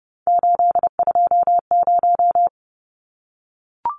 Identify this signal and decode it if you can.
Morse code, heard as a keyed tone, about 30 wpm, 700 Hz. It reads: 820
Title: Numbers - 3-Digit 30wpm